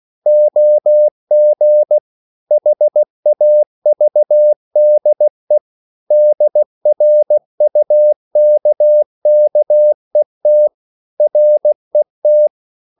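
Morse code, heard as a keyed tone, about 16 wpm, 600 Hz. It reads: OG HAVDE DRUKKET RET
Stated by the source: Slow (QRS) Morse code at 16 wpm, from CW med Gnister.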